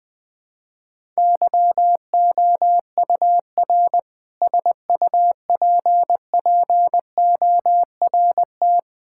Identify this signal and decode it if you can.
Morse code, heard as a keyed tone, about 20 wpm, 700 Hz. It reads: YOUR SUPPORT
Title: Sentences with 7-Word Phrases - Spoken Beginning - 20wpm